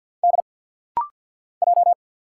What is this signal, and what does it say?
Morse code, heard as a keyed tone, about 50 wpm, 700 Hz. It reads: D J
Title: Single Letters - Rapid-Fire 50wpm